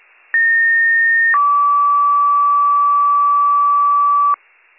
Two-tone page, heard as a one-second tone, then a three-second tone, about 1850 then 1150 Hz, with noise. It is heard from 0.3 to 4.3 s.